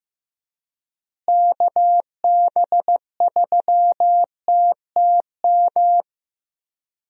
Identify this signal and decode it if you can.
Morse code, heard as a keyed tone, about 15 wpm, 700 Hz. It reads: KB3TTM
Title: USA Call Signs - 6 Characters - Repeated 3x - 15wpm